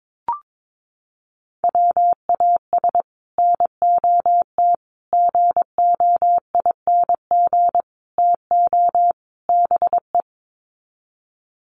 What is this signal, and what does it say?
Morse code, heard as a keyed tone, about 22 wpm, 700 Hz. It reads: WAS NOT GOING TO BE
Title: Sets of 5 Words Encore 22wpm